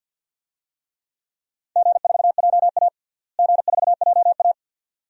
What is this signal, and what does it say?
Morse code, heard as a keyed tone, about 53 wpm, 700 Hz. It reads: G4JA G4JA
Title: DXCC-100 Call Signs - 4 Characters - Repeated 2x - 50wpm